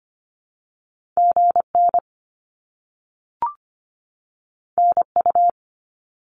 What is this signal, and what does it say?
Morse code, heard as a keyed tone, about 25 wpm, 700 Hz. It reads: GN NU
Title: Top Two Letter Combinations in Words 25wpm